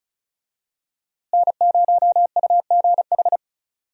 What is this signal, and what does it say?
Morse code, heard as a keyed tone, about 36 wpm, 700 Hz. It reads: N0UGH
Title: USA Call Signs - 5 Characters - Repeated 3x - 35wpm